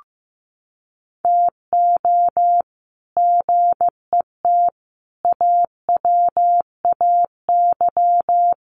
Morse code, heard as a keyed tone, about 15 wpm, 700 Hz. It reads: TO GET AWAY
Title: Sets of 5 Words Encore2 15wpm